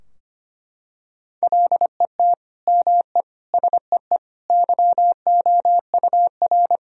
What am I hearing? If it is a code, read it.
Morse code, heard as a keyed tone, about 25 wpm, 700 Hz. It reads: LET ME SEE YOUR
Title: Sentences from Top 1000 Words - Review Part 2 - 25wpm